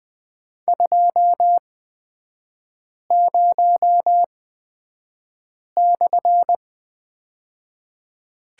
Morse code, heard as a keyed tone, about 20 wpm, spaced as for 5 wpm, 700 Hz. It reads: 20/
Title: 224 Lesson 36e - BT - Three Characters 20wpm